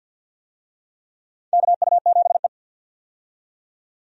Morse code, heard as a keyed tone, about 50 wpm, 700 Hz. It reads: KU7E
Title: USA Call Signs - 4 Characters - 50wpm